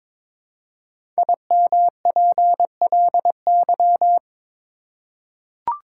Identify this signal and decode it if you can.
Morse code, heard as a keyed tone, about 22 wpm, 700 Hz. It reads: IMPLY
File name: 5 Letter Words - First 3 Letters Given - 22wpm